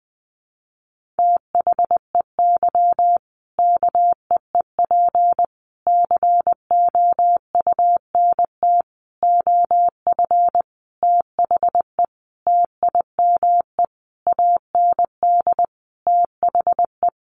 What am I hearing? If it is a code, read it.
Morse code, heard as a keyed tone, about 20 wpm, 700 Hz. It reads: THEY KEEP COUNT OF THE TIME AND THE
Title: Sentences from Top 1500 Words 20wpm